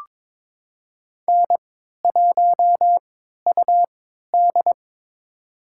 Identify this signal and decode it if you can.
Morse code, heard as a keyed tone, about 22 wpm, spaced as for 12 wpm, 700 Hz. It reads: N1UD
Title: USA Call Signs - 4 Characters - 3x Character Spacing - 22wpm